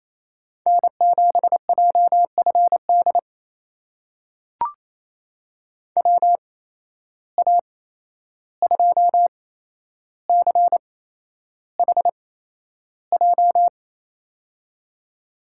Morse code, heard as a keyed tone, about 28 wpm, spaced as for 11 wpm, 700 Hz. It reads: N7JFD WA2CHJ
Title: USA Call Signs - 8x Character Spacing - 28wpm